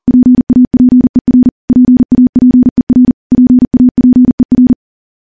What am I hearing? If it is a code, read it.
Morse code, heard as a keyed tone, about 40 wpm, 250 Hz. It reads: PAPER PAPER PAPER
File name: morse / morse-training-40WPM